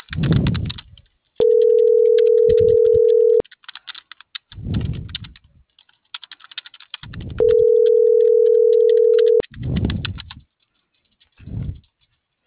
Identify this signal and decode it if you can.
Ringback tone (North America), 440 + 480 Hz.